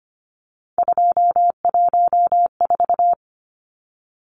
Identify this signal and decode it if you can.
Morse code, heard as a keyed tone, about 25 wpm, 700 Hz. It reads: 214
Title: Numbers - 3-Digit 25wpm